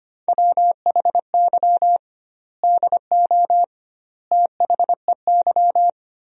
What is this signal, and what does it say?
Morse code, heard as a keyed tone, about 25 wpm, 700 Hz. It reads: WHY DO THEY
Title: 2x Word-Spacing - Sentences from Top 400 Words - 25wpm